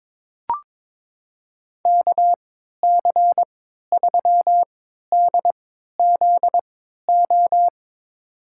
Morse code, heard as a keyed tone, about 22 wpm, spaced as for 12 wpm, 700 Hz. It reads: KC3DZO
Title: USA Call Signs - 6 Characters - 3x Character Spacing - 22wpm